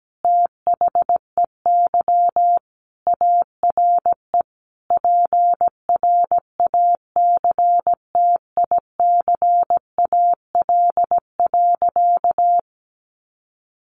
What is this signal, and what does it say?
Morse code, heard as a keyed tone, about 17 wpm, 700 Hz. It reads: THEY ARE PRACTICAL.
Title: Sentences from Top 1400 Words 17wpm